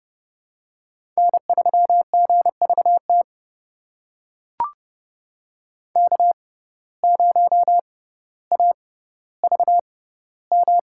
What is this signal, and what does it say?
Morse code, heard as a keyed tone, about 30 wpm, spaced as for 15 wpm, 700 Hz. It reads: N3GVT K0AVM
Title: USA Call Signs - 5 Characters - 6x Character Spacing - 30wpm